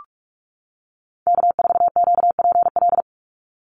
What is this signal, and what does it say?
Morse code, heard as a keyed tone, about 44 wpm, 700 Hz. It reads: K4QPL
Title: Call Signs - 45wpm